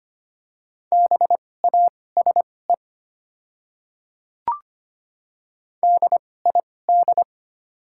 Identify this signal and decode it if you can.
Morse code, heard as a keyed tone, about 25 wpm, spaced as for 18 wpm, 700 Hz. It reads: BASE DID
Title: 2x Character Spacing - Top 500 Words - 25wpm